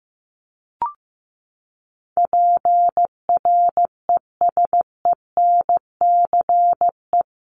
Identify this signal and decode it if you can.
Morse code, heard as a keyed tone, about 15 wpm, 700 Hz. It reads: PRESENCE
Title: Top 1751-2000 Words 15wpm